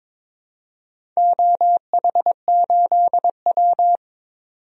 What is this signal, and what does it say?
Morse code, heard as a keyed tone, about 21 wpm, 700 Hz. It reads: OH8W
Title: DXCC-100 Call Signs - 4 Characters - Repeated 2x - 22wpm